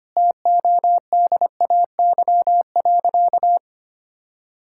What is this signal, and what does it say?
Morse code, heard as a keyed tone, about 25 wpm, 700 Hz. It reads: TODAY.
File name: practice / 25wpm / Sentences with 4-Word Phrases - 25wpm